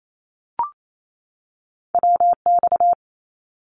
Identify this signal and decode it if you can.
Morse code, heard as a keyed tone, about 28 wpm, 700 Hz. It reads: WX
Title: Abbreviations 28wpm